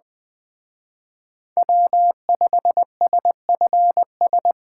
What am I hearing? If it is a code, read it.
Morse code, heard as a keyed tone, about 20 wpm, 700 Hz. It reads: W5SFS
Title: Speed Racing - USA 1x3 Call Signs 17wpm